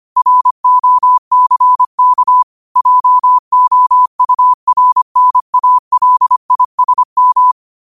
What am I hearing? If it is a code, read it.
Morse code, heard as a keyed tone, about 25 wpm, 1000 Hz. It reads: ROCK JOURNALISM